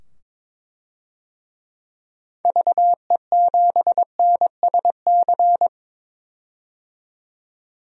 Morse code, heard as a keyed tone, about 22 wpm, 700 Hz. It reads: VE7NSC
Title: DXCC-100 Call Signs - 6 Characters - 22wpm